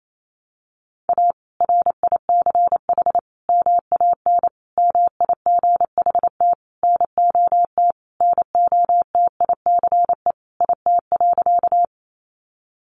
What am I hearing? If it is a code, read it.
Morse code, heard as a keyed tone, about 28 wpm, 700 Hz. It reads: A RICH MAN MIGHT NOT NOTICE IT.
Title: Sentences from Top 1100 Words 28wpm